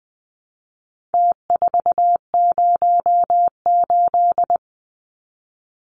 Morse code, heard as a keyed tone, about 20 wpm, 700 Hz. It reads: T408
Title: Numbers - Contest Style - 4-Digit 20wpm